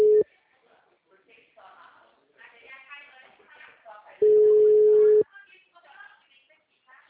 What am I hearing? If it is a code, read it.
Ringback tone (Europe, 425 Hz), with speech.